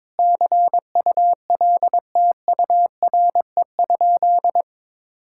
Morse code, heard as a keyed tone, about 22 wpm, 700 Hz. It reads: CULTURE?
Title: Sentences from Top 1000 Words - Review Part 6 - 22wpm